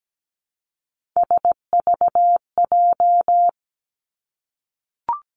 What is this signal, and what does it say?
Morse code, heard as a keyed tone, about 17 wpm, 700 Hz. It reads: SVJ
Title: Three Challenging Letter-Number Combinations 17wpm